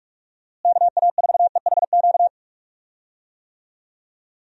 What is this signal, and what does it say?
Morse code, heard as a keyed tone, about 46 wpm, 700 Hz. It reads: KA4EHQ
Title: USA Call Signs - 6 Characters - Repeated 3x - 45wpm